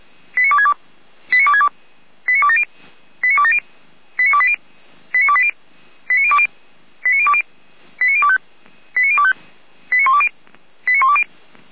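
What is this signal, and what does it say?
Five-tone selective call (ZVEI), heard as quick runs of five tones, tones of about 80 ms. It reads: 89262 89262 89270 89270 89280 89280 89020 89020 89025 89025 89120 89120
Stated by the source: ZVEI five-tone selective call received by SDR.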